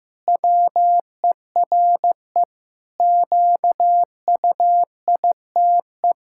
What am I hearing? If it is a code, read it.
Morse code, heard as a keyed tone, about 15 wpm, 700 Hz. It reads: WERE QUITE
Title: Pangram Challenge v2 - 15wpm